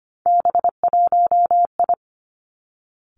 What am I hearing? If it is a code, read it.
Morse code, heard as a keyed tone, about 25 wpm, 700 Hz. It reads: B1I
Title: Three Challenging Letter-Number Combinations 25wpm